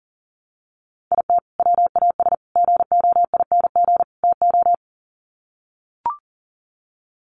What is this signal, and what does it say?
Morse code, heard as a keyed tone, about 40 wpm, 700 Hz. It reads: IT WAS GOING TO
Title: Sets of 4 Words 40wpm